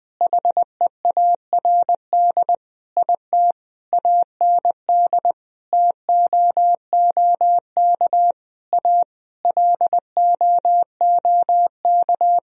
Morse code, heard as a keyed tone, about 20 wpm, 700 Hz. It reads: HEARD IT AND TOOK A LOOK
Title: Sentences from Top 1000 Words - Review Part 1 - 20wpm